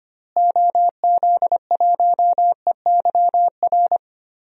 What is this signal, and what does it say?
Morse code, heard as a keyed tone, about 25 wpm, 700 Hz. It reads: OZ1EYR
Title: DXCC-100 Call Signs - 6 Characters - Repeated 2x - 25wpm